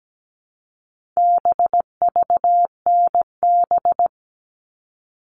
Morse code, heard as a keyed tone, about 17 wpm, 700 Hz. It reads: BVNB
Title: All Cut Numbers - 4-Digit 17wpm